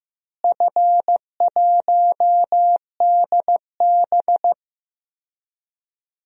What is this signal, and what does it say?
Morse code, heard as a keyed tone, about 16 wpm, 700 Hz. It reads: F1DB
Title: DXCC-100 Call Signs - 4 Characters - Repeated 2x - 15wpm